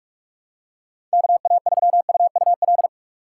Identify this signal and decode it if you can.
Morse code, heard as a keyed tone, about 46 wpm, 700 Hz. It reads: KA3UUL
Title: USA Call Signs - 6 Characters - 6x Character Spacing - 45wpm